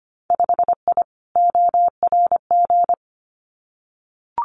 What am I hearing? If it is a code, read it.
Morse code, heard as a keyed tone, about 25 wpm, 700 Hz. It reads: 5I ORG
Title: Winter Field Day Exchanges Encore 25wpm